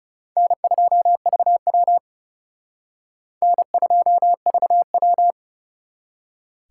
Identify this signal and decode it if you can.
Morse code, heard as a keyed tone, about 37 wpm, 700 Hz. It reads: N2VW N2VW
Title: Speed Racing - USA 1x2 Call Signs 30wpm